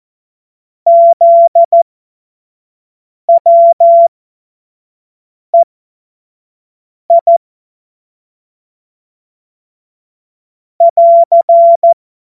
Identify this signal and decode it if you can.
Morse code, heard as a keyed tone, about 14 wpm, spaced as for 5 wpm, 680 Hz. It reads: ZWEI <AR>
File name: Morsekurs / Witze / Witz 06. Januar 2026 25 F70